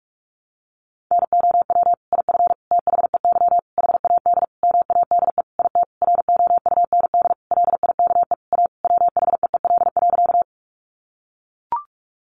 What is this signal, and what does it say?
Morse code, heard as a keyed tone, about 44 wpm, 700 Hz. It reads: NOW IF THEY HAD MADE IT ROUND LIKE A WHEEL.
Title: Sentences from Top 500 Words 45wpm